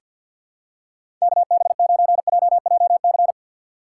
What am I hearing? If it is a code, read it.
Morse code, heard as a keyed tone, about 50 wpm, 700 Hz. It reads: KB9JJC
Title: USA Call Signs - 4x Character Spacing - 50wpm